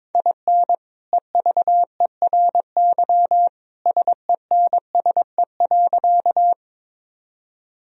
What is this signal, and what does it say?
Morse code, heard as a keyed tone, about 22 wpm, 700 Hz. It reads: IN EVERY SENSE.
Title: Sentences from Top 1000 Words - Review Part 4 - 22wpm